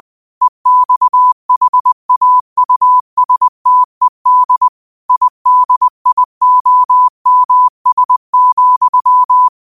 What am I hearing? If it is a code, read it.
Morse code, heard as a keyed tone, about 20 wpm, 1000 Hz. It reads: EXHAUSTED IDIOMS,
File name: qotdfiles / 20wpm